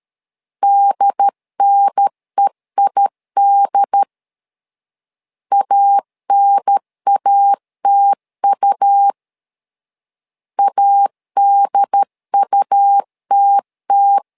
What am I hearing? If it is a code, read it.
Morse code, heard as a keyed tone, about 13 wpm, 800 Hz. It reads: DNEID ANATU ADUTT
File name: LEZIONI / 2 UHDN / 2 UHDN 12wpm 1456